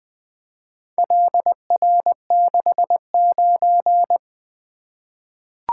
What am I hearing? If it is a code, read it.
Morse code, heard as a keyed tone, about 20 wpm, 700 Hz. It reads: LR69
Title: Four Letter-Number Combinations - 20wpm